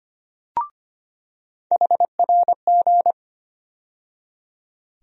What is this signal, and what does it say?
Morse code, heard as a keyed tone, about 25 wpm, 700 Hz. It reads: HRG
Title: USA Call Sign Postfixes - 3 Letters 25wpm